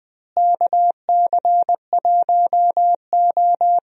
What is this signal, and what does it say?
Morse code, heard as a keyed tone, about 20 wpm, 700 Hz. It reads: KC1O
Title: USA Call Signs - 4 Characters - Repeated 2x - 20wpm